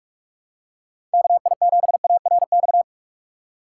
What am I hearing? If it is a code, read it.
Morse code, heard as a keyed tone, about 45 wpm, 700 Hz. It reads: KI7ARX
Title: USA Call Signs - 6 Characters - 4x Character Spacing - 45wpm